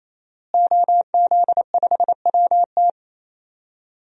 Morse code, heard as a keyed tone, about 28 wpm, 700 Hz. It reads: OZ5WT
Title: DXCC-100 Call Signs - 5 Characters - Repeated 3x - 28wpm